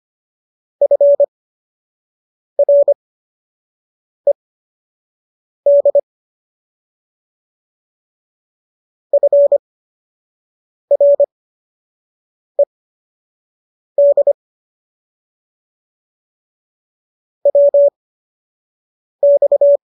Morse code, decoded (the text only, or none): FRED FRED WX